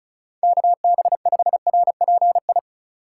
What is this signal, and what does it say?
Morse code, heard as a keyed tone, about 45 wpm, 700 Hz. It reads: KB5RPI